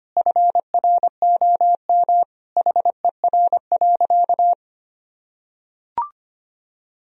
Morse code, heard as a keyed tone, about 25 wpm, 700 Hz. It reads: FROM HER.
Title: Sentences from Top 1000 Words - Review Part 2 - 25wpm